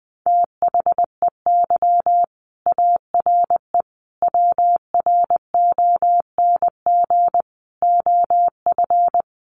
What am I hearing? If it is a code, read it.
Morse code, heard as a keyed tone, about 20 wpm, 700 Hz. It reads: THEY ARE WRONG OF
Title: Sentences from Top 900 Words 20wpm